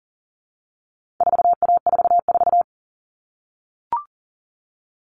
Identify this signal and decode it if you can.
Morse code, heard as a keyed tone, about 40 wpm, 700 Hz. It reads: 4A44
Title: All Cut Numbers - 4-Digit 40wpm